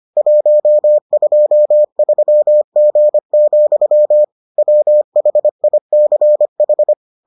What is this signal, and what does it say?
Morse code, heard as a keyed tone, about 25 wpm, 600 Hz. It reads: 123G, WHICH